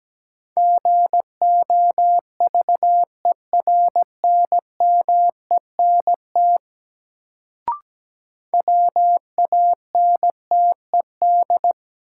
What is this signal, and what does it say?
Morse code, heard as a keyed tone, about 17 wpm, 700 Hz. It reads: GOVERNMENT WANTED
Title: Sets of 2 Words - New Word Drill - 17wpm